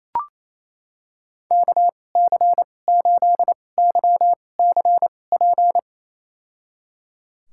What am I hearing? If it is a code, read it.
Morse code, heard as a keyed tone, about 28 wpm, spaced as for 20 wpm, 700 Hz. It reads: KC8YCP
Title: USA Call Signs - 6 Characters - 2x Character Spacing - 28wpm